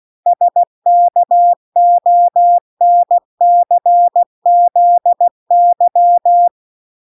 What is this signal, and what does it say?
Morse code, heard as a keyed tone, about 16 wpm, 700 Hz. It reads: SKONCZY